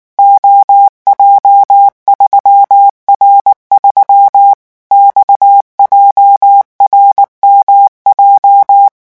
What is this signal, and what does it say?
Morse code, heard as a keyed tone, about 19 wpm, 800 Hz. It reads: OJ3R3 XJRMJ
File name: morse mp3 / revision12-vitesse 18 mots